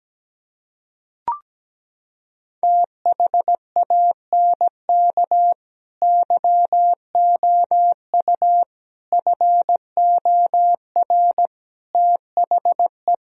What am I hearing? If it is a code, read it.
Morse code, heard as a keyed tone, about 17 wpm, 700 Hz. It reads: THANK YOU FOR THE
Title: Sentences from Top 1000 Words - Review Part 4 - 17wpm